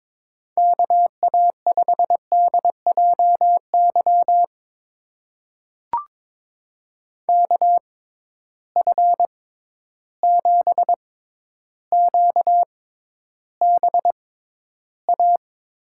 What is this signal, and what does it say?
Morse code, heard as a keyed tone, about 22 wpm, spaced as for 11 wpm, 700 Hz. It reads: KA5DJY KF7QBA
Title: USA Call Signs - 6x Character Spacing - 22wpm